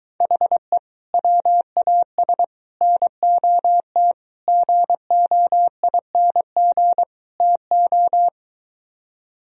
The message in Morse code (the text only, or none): HE WAS NOT GOING TO